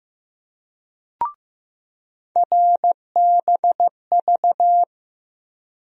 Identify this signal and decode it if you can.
Morse code, heard as a keyed tone, about 15 wpm, 700 Hz. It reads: RBV
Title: USA Call Sign Postfixes - 3 Letters 15wpm